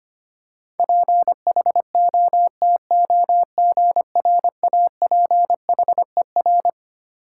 Morse code, heard as a keyed tone, about 25 wpm, 700 Hz. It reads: PHOTOGRAPHER
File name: Twelve Letter Words - 25wpm